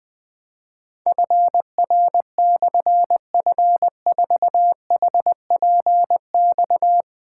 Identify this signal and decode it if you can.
Morse code, heard as a keyed tone, about 20 wpm, 700 Hz. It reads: FR/F4HPX
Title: Difficult Call Signs 20wpm